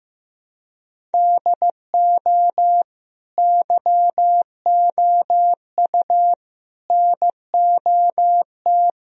Morse code, heard as a keyed tone, about 15 wpm, 700 Hz. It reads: DO YOU NOT
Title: Sentences from Top 1000 Words - Review Part 1 - 15wpm